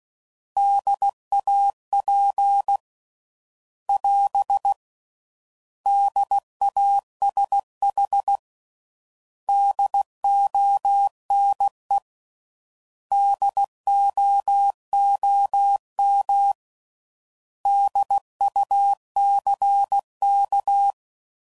Morse code, heard as a keyed tone, about 16 wpm, 800 Hz. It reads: DAP <AS> DASH DONE DOOM DUCK
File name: Prefixes / By First Letter / Ds